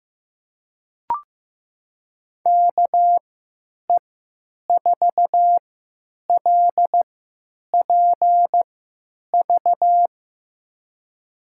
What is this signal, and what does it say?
Morse code, heard as a keyed tone, about 15 wpm, spaced as for 9 wpm, 700 Hz. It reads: KE4LPV
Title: USA Call Signs - 6 Characters - 3x Character Spacing - 15wpm